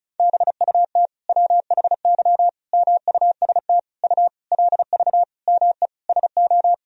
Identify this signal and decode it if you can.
Morse code, heard as a keyed tone, about 36 wpm, 700 Hz. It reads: BUT WHY MUST U LV ME SO
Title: Sentences from Top 400 Words and Abbreviations 35wpm